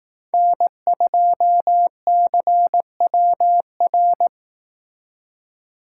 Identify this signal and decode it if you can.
Morse code, heard as a keyed tone, about 18 wpm, 700 Hz. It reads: N2CWR
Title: Speed Racing - USA 1x3 Call Signs 15wpm